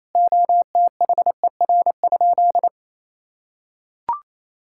Morse code, heard as a keyed tone, about 28 wpm, 700 Hz. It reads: OTHER?
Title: Sentences from Top 1000 Words - Review Part 6 - 28wpm